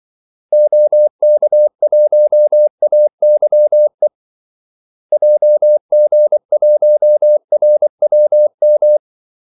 Morse code, heard as a keyed tone, about 24 wpm, 600 Hz. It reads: OK1AYE JG1RWM
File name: Callsigns-24wpm